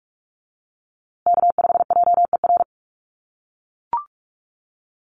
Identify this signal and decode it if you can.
Morse code, heard as a keyed tone, about 44 wpm, 690 Hz. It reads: K5JER
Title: USA Call Signs Repeated 3x 45wpm